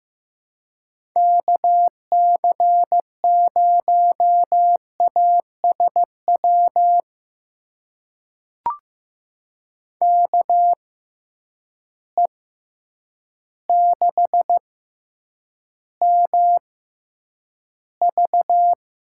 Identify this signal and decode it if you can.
Morse code, heard as a keyed tone, about 15 wpm, spaced as for 8 wpm, 700 Hz. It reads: KC0ASW KE6MV
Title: USA Call Signs - 6 Characters - 6x Character Spacing - 15wpm